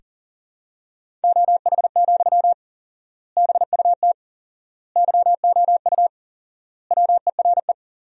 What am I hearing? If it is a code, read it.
Morse code, heard as a keyed tone, about 40 wpm, 700 Hz. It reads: OH, BUT YOU WERE
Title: 4x Word-Spacing - Sentences from Top 500 Words - 40wpm